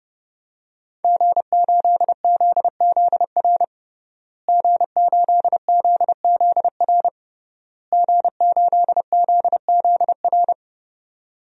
Morse code, heard as a keyed tone, about 30 wpm, 700 Hz. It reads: G8ZZR G8ZZR G8ZZR
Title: DXCC-100 Call Signs - 5 Characters - Repeated 3x - 30wpm